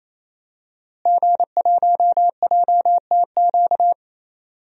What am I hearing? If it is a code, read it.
Morse code, heard as a keyed tone, about 28 wpm, 700 Hz. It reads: G1JTQ
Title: DXCC-100 Call Signs - 5 Characters - Repeated 3x - 28wpm